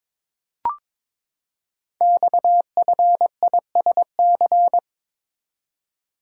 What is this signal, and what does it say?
Morse code, heard as a keyed tone, about 22 wpm, 700 Hz. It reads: XFISC
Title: Code Groups - 5 Letters - 22wpm